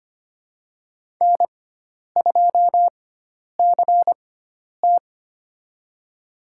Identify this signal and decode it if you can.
Morse code, heard as a keyed tone, about 25 wpm, spaced as for 10 wpm, 700 Hz. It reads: N2CT
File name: Lesson 16g - Letter c - Call Signs 25wpm